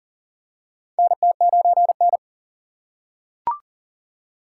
Morse code, heard as a keyed tone, about 40 wpm, 700 Hz. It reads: NT9N
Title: USA Call Signs - 4 Characters - 6x Character Spacing - 40wpm